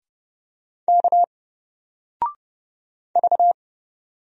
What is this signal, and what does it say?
Morse code, heard as a keyed tone, about 31 wpm, 700 Hz. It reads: K V
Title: Single Character (Letter:Number:Punctuation:Prosign) 30wpm